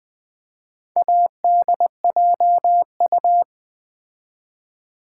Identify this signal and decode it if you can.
Morse code, heard as a keyed tone, about 20 wpm, 700 Hz. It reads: ADJU
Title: Build A Word - 4 Letters At A Time - 10-Letter Words - 20wpm